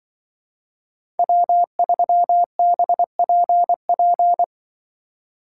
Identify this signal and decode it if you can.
Morse code, heard as a keyed tone, about 25 wpm, 700 Hz. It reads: W3BPP
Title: Speed Racing - USA Call Signs 20wpm